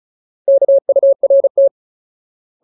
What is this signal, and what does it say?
Morse code, heard as a keyed tone, about 34 wpm, 550 Hz. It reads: KURT
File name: Ultimate Word List / 35WPM Ultimate